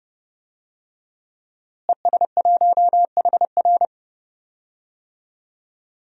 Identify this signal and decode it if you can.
Morse code, heard as a keyed tone, about 30 wpm, 700 Hz. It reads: ES1HR
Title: DXCC-100 Call Signs - 5 Characters - Repeated 3x - 30wpm